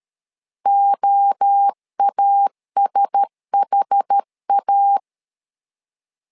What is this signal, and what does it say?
Morse code, heard as a keyed tone, about 13 wpm, 800 Hz. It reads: OASHA
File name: LEZIONI / 4 Q1S3 12wpm 2023